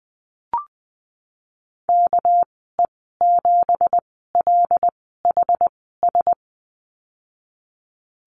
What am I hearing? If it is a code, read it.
Morse code, heard as a keyed tone, about 20 wpm, spaced as for 15 wpm, 700 Hz. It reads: KE7LHS
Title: 231 Lesson 37g - Letter x - Call Signs 20wpm